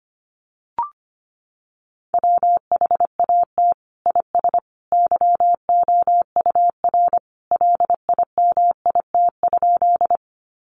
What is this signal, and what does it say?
Morse code, heard as a keyed tone, about 25 wpm, 700 Hz. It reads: WHAT IS YOUR LIMIT?
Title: Sentences from Top 1500 Words 25wpm